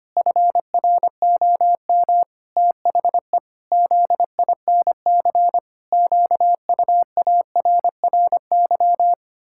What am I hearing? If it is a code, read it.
Morse code, heard as a keyed tone, about 25 wpm, 700 Hz. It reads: FROM THE ZINC QUARRY
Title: Pangram Challenge v2 - 25wpm